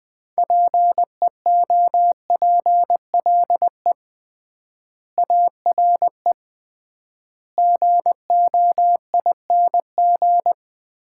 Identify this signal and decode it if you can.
Morse code, heard as a keyed tone, about 21 wpm, 700 Hz. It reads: PEOPLE ARE GOING
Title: 3x Word-Spacing - Sets of 4 Words - 20wpm